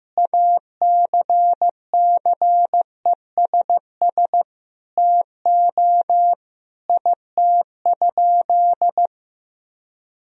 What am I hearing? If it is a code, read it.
Morse code, heard as a keyed tone, about 15 wpm, 700 Hz. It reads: ACCESS TO IT?
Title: Sentences from Top 1000 Words - Review Part 4 - 15wpm